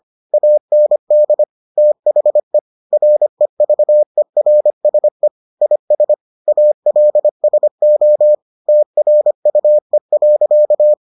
Morse code, heard as a keyed tone, about 25 wpm, 600 Hz. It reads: AND THE REVERSE IS ALSO TRUE.